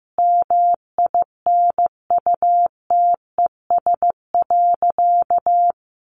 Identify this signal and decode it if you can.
Morse code, heard as a keyed tone, about 15 wpm, 700 Hz. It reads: MINUTES.
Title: A Sound Of Thunder 15wpm